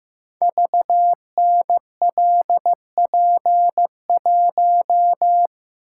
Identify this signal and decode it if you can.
Morse code, heard as a keyed tone, about 15 wpm, 700 Hz. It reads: VNLP1
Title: Code Groups - 5 Letter:Numbers - 15wpm